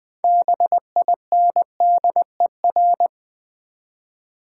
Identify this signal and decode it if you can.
Morse code, heard as a keyed tone, about 20 wpm, 700 Hz. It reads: BINDER
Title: Easy Words with Hints - 20wpm